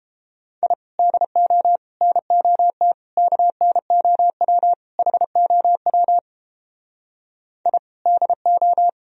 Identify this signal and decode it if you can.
Morse code, heard as a keyed tone, about 29 wpm, 700 Hz. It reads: I DO NOT KNOW HOW I DO